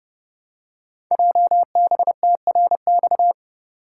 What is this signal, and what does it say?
Morse code, heard as a keyed tone, about 29 wpm, 700 Hz. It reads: JBTRX
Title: Code Groups - 5 Letters - 30wpm